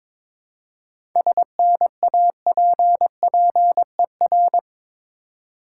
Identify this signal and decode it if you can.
Morse code, heard as a keyed tone, about 22 wpm, 700 Hz. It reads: SNAPPER
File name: Difficult Words with Hints - 22wpm